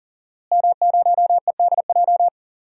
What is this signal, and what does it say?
Morse code, heard as a keyed tone, about 40 wpm, 700 Hz. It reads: M0EDJ
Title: DXCC-100 Call Signs - 5 Characters - Repeated 3x - 40wpm